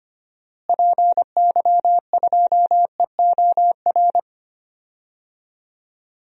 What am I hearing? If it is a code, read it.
Morse code, heard as a keyed tone, about 25 wpm, 700 Hz. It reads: PY2EOR
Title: DXCC-100 Call Signs - 6 Characters - Repeated 2x - 25wpm